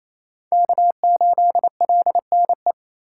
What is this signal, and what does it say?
Morse code, heard as a keyed tone, about 28 wpm, 700 Hz. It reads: K8LNE
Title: USA Call Signs - 2x Character Spacing - 28wpm